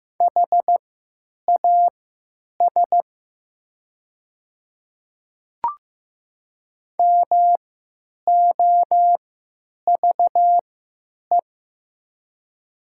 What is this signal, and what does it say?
Morse code, heard as a keyed tone, about 15 wpm, spaced as for 8 wpm, 700 Hz. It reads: HAS MOVE
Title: 3x Character Spacing - Top 200 Words - 15wpm